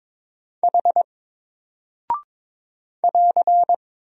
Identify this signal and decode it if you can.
Morse code, heard as a keyed tone, about 22 wpm, 700 Hz. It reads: H <AR>
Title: Single Character (Letter:Number:Punctuation:Prosign) 22wpm